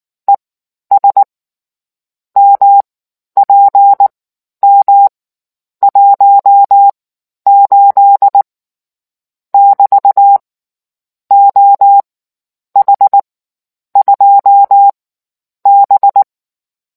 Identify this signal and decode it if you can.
Morse code, heard as a keyed tone, about 19 wpm, spaced as for 10 wpm, 800 Hz. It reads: ES MPM 18 <BT> OH2B